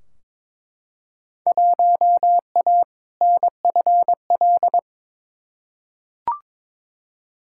Morse code, heard as a keyed tone, about 22 wpm, 700 Hz. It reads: 1A NFL
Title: Field Day Exchanges Encore - 22wpm